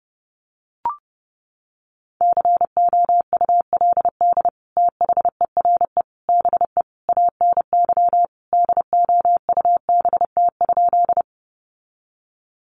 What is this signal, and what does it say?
Morse code, heard as a keyed tone, about 30 wpm, 700 Hz. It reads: COULD THERE BE ANY DOUBT?
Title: Sentences from Top 1500 Words 30wpm